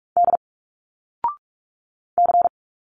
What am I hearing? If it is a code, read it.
Morse code, heard as a keyed tone, about 46 wpm, 700 Hz. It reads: D C
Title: Single Letters 45wpm